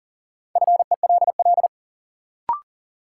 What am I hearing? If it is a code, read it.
Morse code, heard as a keyed tone, about 40 wpm, 700 Hz. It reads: FELL
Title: Sets of 5 Words Encore - New Word Drill - 40wpm